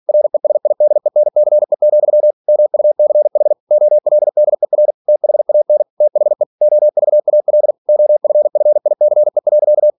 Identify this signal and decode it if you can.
Morse code, heard as a keyed tone, about 46 wpm, 600 Hz. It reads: RESIDENCE, MUCH OLDER THAN THE OVAL OFFICE.
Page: Never 47wpm